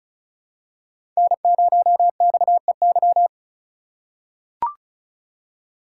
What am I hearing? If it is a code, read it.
Morse code, heard as a keyed tone, about 36 wpm, 700 Hz. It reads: N0XEY
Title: USA Call Signs - 5 Characters - 3x Character Spacing - 35wpm